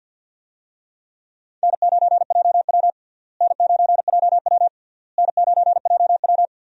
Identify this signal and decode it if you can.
Morse code, heard as a keyed tone, about 49 wpm, 700 Hz. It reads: N9JW N9JW N9JW
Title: USA Call Signs - 4 Characters - Repeated 3x - 50wpm